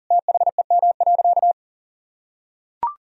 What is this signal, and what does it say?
Morse code, heard as a keyed tone, about 40 wpm, 700 Hz. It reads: THEM.
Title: Sentences from Top 1000 Words - Review Part 1 - 40wpm